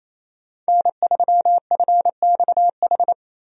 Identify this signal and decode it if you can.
Morse code, heard as a keyed tone, about 28 wpm, 700 Hz. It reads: N3FXH
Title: USA Call Signs - 5 Characters - Repeated 2x - 28wpm